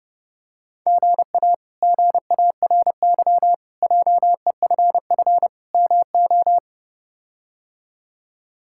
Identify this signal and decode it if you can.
Morse code, heard as a keyed tone, about 32 wpm, 700 Hz. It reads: GA GARY JEFF MO
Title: K1USN SST QSO Practice - 30wpm